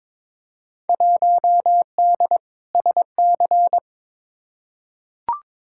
Morse code, heard as a keyed tone, about 22 wpm, 700 Hz. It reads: 1D SC
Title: Field Day Exchanges - 22wpm